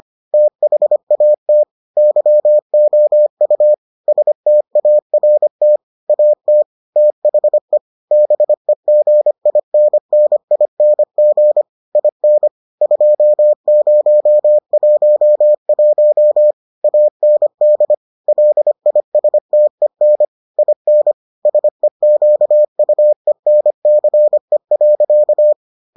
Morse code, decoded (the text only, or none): THAT YOU START AT THE BEGINNING IN 2011 AND LISTEN IN SEQUENCE.